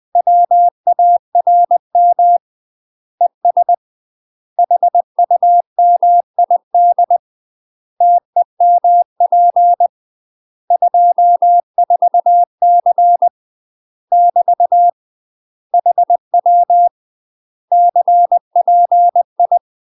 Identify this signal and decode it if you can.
Morse code, heard as a keyed tone, about 20 wpm, 700 Hz. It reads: WARM ES HUMID TEMP 24C <BT> HW CPI
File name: UK MASTER QSOs TRAINING/20WPM